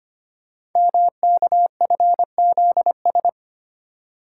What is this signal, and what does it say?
Morse code, heard as a keyed tone, about 25 wpm, 700 Hz. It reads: MKFZS
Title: Code Groups - 5 Letters - 25wpm